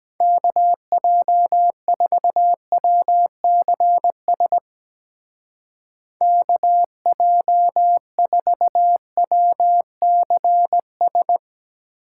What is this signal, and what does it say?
Morse code, heard as a keyed tone, about 22 wpm, 700 Hz. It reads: KJ4WCS KJ4WCS